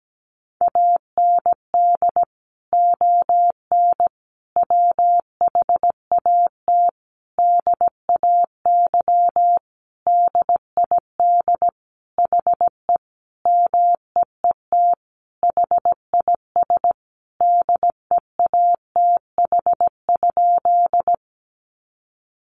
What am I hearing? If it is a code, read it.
Morse code, heard as a keyed tone, about 17 wpm, 700 Hz. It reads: AND ON WHAT DAY DID HE MEET HIS DEATH?
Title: Sentences from Top 1100 Words 17wpm